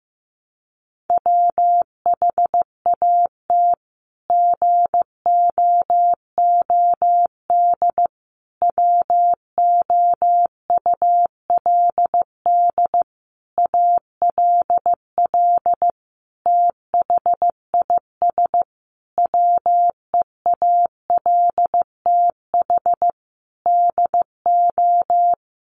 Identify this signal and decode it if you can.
Morse code, heard as a keyed tone, about 15 wpm, 700 Hz. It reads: WHAT GOOD WOULD ALL THIS WEALTH DO
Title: Sentences from Top 1200 Words 15wpm